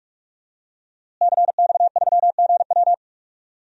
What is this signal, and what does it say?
Morse code, heard as a keyed tone, about 46 wpm, 700 Hz. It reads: CX3GW
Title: DXCC-100 Call Signs - 5 Characters - Repeated 2x - 45wpm